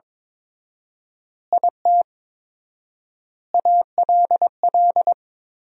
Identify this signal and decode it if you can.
Morse code, heard as a keyed tone, about 22 wpm, 700 Hz. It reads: IT ALL
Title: 4x Word-Spacing - Sentences from Top 500 Words - 22wpm